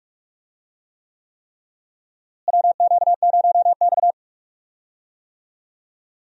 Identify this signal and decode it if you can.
Morse code, heard as a keyed tone, about 47 wpm, 700 Hz. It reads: WQ0X
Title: USA Call Signs - 4 Characters - Repeated 3x - 45wpm